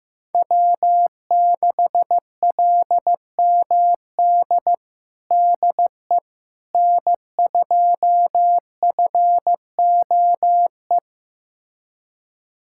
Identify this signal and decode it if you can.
Morse code, heard as a keyed tone, about 15 wpm, 700 Hz. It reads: W6LMD DE N2FOE
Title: USA Callsign DE Callsign - 5 Characters - 15wpm